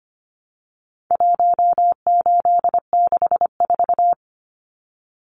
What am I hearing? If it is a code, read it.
Morse code, heard as a keyed tone, about 25 wpm, 700 Hz. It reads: 1864
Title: Numbers - 4-Digit 25wpm